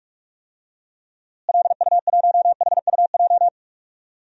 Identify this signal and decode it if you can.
Morse code, heard as a keyed tone, about 46 wpm, 700 Hz. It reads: LU1HUJ
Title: DXCC-100 Call Signs - 6 Characters - Repeated 3x - 45wpm